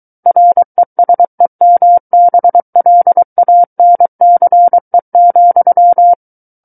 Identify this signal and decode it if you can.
Morse code, heard as a keyed tone, about 23 wpm, 700 Hz. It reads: RESEMBLANCE,
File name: Book One / 14-Chapter 14 - 23 WPM